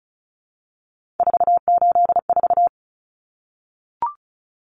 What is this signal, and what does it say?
Morse code, heard as a keyed tone, about 34 wpm, 700 Hz. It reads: 484
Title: Numbers - 3-Digit 35wpm